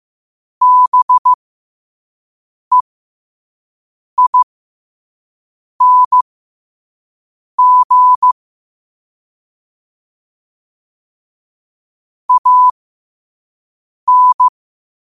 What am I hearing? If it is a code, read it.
Morse code, heard as a keyed tone, about 15 wpm, spaced as for 5 wpm, 1000 Hz. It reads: BEING AN
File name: qotdfiles / 5wpm